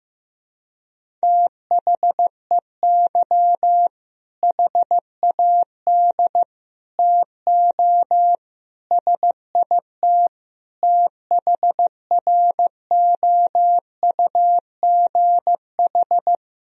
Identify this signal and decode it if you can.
Morse code, heard as a keyed tone, about 16 wpm, 700 Hz. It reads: THEY HAD TO SIT THROUGH
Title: Sentences from Top 1000 Words - Review Part 1 - 15wpm